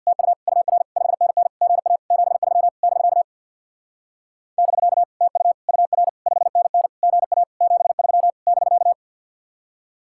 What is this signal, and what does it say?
Morse code, heard as a keyed tone, about 62 wpm, 700 Hz. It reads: TU UR 5NN GA 73 <BK> <BK> TU UR 5NN GA 73 <BK>
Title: Speed-Racing Common QSO Phrases - 50wpm